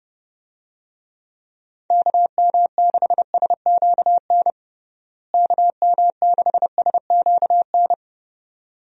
Morse code, heard as a keyed tone, about 32 wpm, 700 Hz. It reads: KM6SQN KM6SQN
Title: USA Call Signs - 6 Characters - Repeated 2x - 30wpm